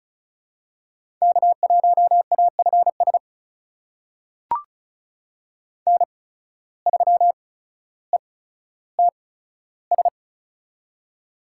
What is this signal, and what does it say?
Morse code, heard as a keyed tone, about 36 wpm, spaced as for 15 wpm, 700 Hz. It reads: K1AFS N3ETS
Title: USA Call Signs - 5 Characters - 8x Character Spacing - 35wpm